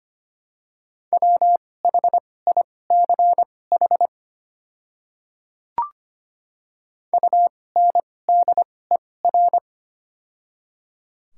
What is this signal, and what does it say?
Morse code, heard as a keyed tone, about 25 wpm, spaced as for 18 wpm, 700 Hz. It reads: WHICH UNDER
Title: 2x Character Spacing - Top 300 Words - 25wpm